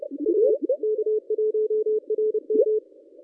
Morse code, heard as a keyed tone, about 29 wpm, 430 Hz. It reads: A K1RM
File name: contests / ss-cw-k5tr-op-n5rz -1200